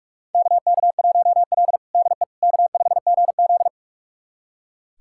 Morse code, heard as a keyed tone, about 44 wpm, 690 Hz. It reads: KK1L DE K5GZ
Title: USA Callsign DE Callsign - 4 Characters - 45wpm